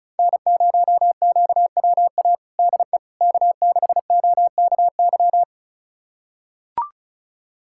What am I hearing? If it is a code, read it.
Morse code, heard as a keyed tone, about 34 wpm, 700 Hz. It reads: N0QWA DE K6OKY
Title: USA Callsign DE Callsign - 5 Characters - 35wpm